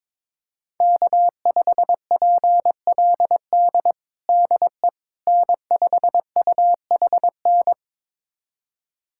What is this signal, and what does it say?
Morse code, heard as a keyed tone, about 22 wpm, 700 Hz. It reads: K5PLD DE N5UHN
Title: USA Callsign DE Callsign - 5 Characters - 22wpm